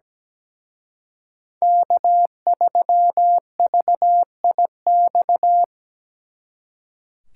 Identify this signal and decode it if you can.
Morse code, heard as a keyed tone, about 20 wpm, 700 Hz. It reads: K3VIX
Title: Speed Racing - USA 1x3 Call Signs 17wpm